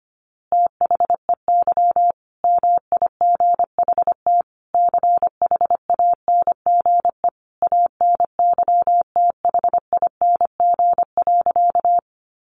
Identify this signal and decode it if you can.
Morse code, heard as a keyed tone, about 25 wpm, 700 Hz. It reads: THEY MIGHT CHANGE ANYTHING.
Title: A Sound Of Thunder 25wpm